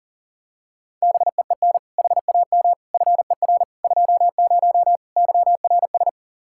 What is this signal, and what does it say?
Morse code, heard as a keyed tone, about 40 wpm, 700 Hz. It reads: BEEN HAM FER 20 YRS
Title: Practice QSO - Exchange 3 - More Info - 40wpm